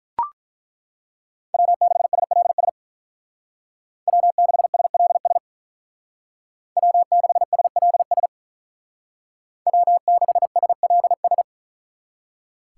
Morse code, heard as a keyed tone, about 43 wpm, 700 Hz. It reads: W6SLS W6SLS W6SLS W6SLS
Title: Speed Racing - USA 1x3 Call Signs 35wpm